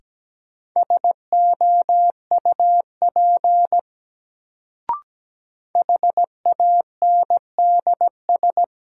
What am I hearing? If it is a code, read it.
Morse code, heard as a keyed tone, about 17 wpm, 700 Hz. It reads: SOUP HANDS
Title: Sentences from Top 1000 Words - Review Part 6 - New Word Drill - 17wpm